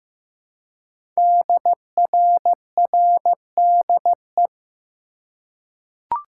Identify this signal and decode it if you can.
Morse code, heard as a keyed tone, about 15 wpm, 700 Hz. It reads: DRRDE
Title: Code Groups - 5 Letters - 15wpm